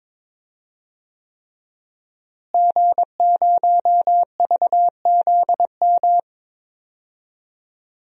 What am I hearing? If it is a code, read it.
Morse code, heard as a keyed tone, about 22 wpm, 700 Hz. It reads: G0VZM
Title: DXCC-100 Call Signs - 5 Characters - Repeated 3x - 22wpm